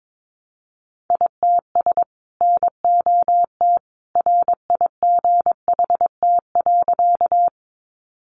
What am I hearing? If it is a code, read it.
Morse code, heard as a keyed tone, about 22 wpm, 700 Hz. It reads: ITS NOT RIGHT.
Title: Sentences from Top 600 Words 22wpm